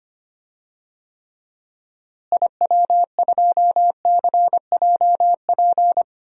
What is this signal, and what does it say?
Morse code, heard as a keyed tone, about 25 wpm, 700 Hz. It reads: IW2CJP
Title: DXCC-100 Call Signs - 6 Characters - Repeated 3x - 25wpm